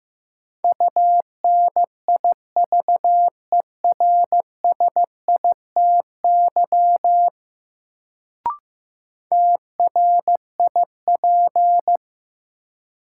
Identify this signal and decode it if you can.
Morse code, heard as a keyed tone, about 15 wpm, 700 Hz. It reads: UNIVERSITY TRIP
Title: Sentences from Top 1000 Words - Review Part 4 - New Word Drill - 15wpm